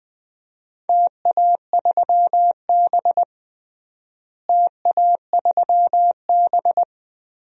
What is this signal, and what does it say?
Morse code, heard as a keyed tone, about 21 wpm, 700 Hz. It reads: TA3B TA3B
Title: DXCC-100 Call Signs - 4 Characters - Repeated 2x - 20wpm